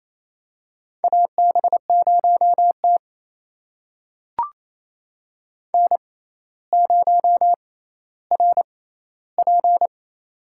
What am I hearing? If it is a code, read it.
Morse code, heard as a keyed tone, about 28 wpm, spaced as for 14 wpm, 700 Hz. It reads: AB0T N0RP
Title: USA Call Signs - 4 Characters - 6x Character Spacing - 28wpm